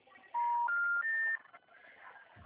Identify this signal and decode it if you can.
Special information tone.
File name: out- -2014